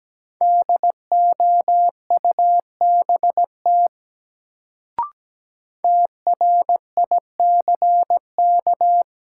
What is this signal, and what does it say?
Morse code, heard as a keyed tone, about 17 wpm, 700 Hz. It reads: DOUBT TRICK
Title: Sentences from Top 1500 Words - New Word Drill - 17wpm